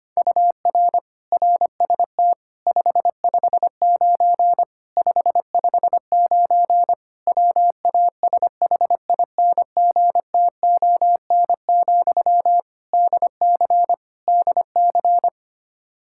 Morse code, heard as a keyed tone, about 25 wpm, 700 Hz. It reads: UR RST 559 559 WASHINGTON, DC DC
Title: Practice QSO - Exchange 1 - Beginning - 25wpm